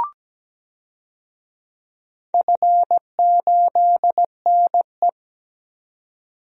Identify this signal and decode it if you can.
Morse code, heard as a keyed tone, about 17 wpm, 700 Hz. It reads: F8NE
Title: DXCC-100 Call Signs - 4 Characters - 17wpm